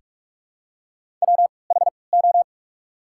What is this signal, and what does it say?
Morse code, heard as a keyed tone, about 48 wpm, spaced as for 26 wpm, 700 Hz. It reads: WHO
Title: 3x Character Spacing - Top 500 Words - 45wpm